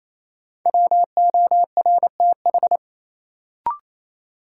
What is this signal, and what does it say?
Morse code, heard as a keyed tone, about 28 wpm, 700 Hz. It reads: WORTH
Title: Sentences from Top 800 Words - New Word Drill - 28wpm